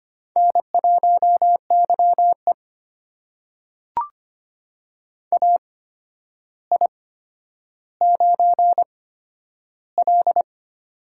Morse code, heard as a keyed tone, about 25 wpm, spaced as for 11 wpm, 700 Hz. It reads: N1YE AI9L
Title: USA Call Signs - 4 Characters - 8x Character Spacing - 25wpm